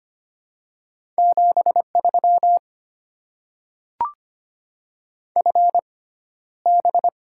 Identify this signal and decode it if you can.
Morse code, heard as a keyed tone, about 25 wpm, spaced as for 13 wpm, 700 Hz. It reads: 73 FB
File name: Top 20 QSO Elements - 6x Character Spacing - 25wpm